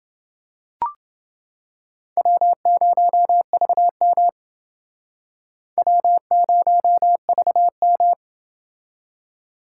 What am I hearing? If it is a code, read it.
Morse code, heard as a keyed tone, about 30 wpm, 700 Hz. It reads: W0VM W0VM